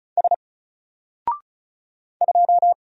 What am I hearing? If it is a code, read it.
Morse code, heard as a keyed tone, about 36 wpm, 700 Hz. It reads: S 2
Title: Single Character (Letter:Number:Punctuation:Prosign) Alt 35wpm